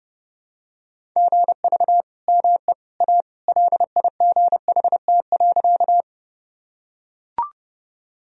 Morse code, heard as a keyed tone, about 32 wpm, 700 Hz. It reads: GV ME A LIGHT.
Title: Sentences from Top 200 Words and Abbreviations 30wpm